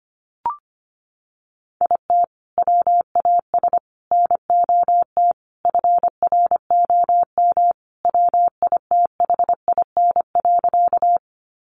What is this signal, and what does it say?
Morse code, heard as a keyed tone, about 25 wpm, 700 Hz. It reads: IT WAS NOT FROM WITHIN.
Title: Sentences from Top 700 Words 25wpm